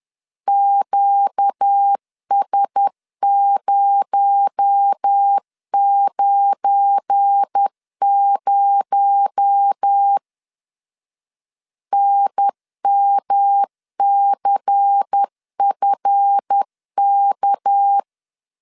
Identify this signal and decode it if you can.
Morse code, heard as a keyed tone, about 10 wpm, 800 Hz. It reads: QS090 NMCFK
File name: LEZIONI / 7 GF9Z / 7 GF9 10wpm 2747